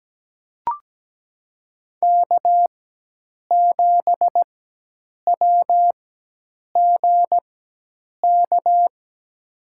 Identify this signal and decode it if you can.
Morse code, heard as a keyed tone, about 17 wpm, spaced as for 8 wpm, 700 Hz. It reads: K7WGK
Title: USA Call Signs - 5 Characters - 4x Character Spacing - 17wpm